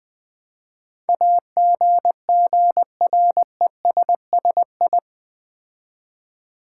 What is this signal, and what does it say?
Morse code, heard as a keyed tone, about 20 wpm, 700 Hz. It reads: AGGRESSI
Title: Build A Word - 4 Letters At A Time - 10-Letter Words - 20wpm